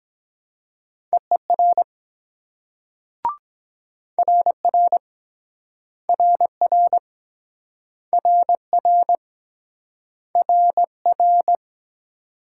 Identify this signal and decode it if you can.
Morse code, heard as a keyed tone, about 23 wpm, 690 Hz. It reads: EER RR RR RR RR